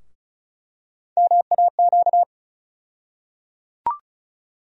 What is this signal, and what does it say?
Morse code, heard as a keyed tone, about 34 wpm, 700 Hz. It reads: MAQ
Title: USA Call Sign Postfixes - 3 Letters 35wpm